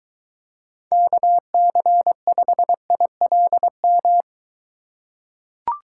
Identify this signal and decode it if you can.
Morse code, heard as a keyed tone, about 23 wpm, 700 Hz. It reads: KC5ILM